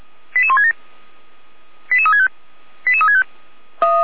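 Five-tone selective call (ZVEI), heard as quick runs of five tones, tones of about 70 ms. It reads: 88167 88365 88365
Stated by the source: ZVEI five-tone call 88168 received by SDR, channel 455.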